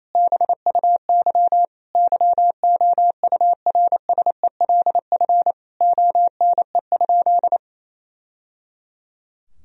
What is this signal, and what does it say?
Morse code, heard as a keyed tone, about 28 wpm, 700 Hz. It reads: BUY YOURSELF ONE?
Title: Sentences from Top 1000 Words - Review Part 4 - 28wpm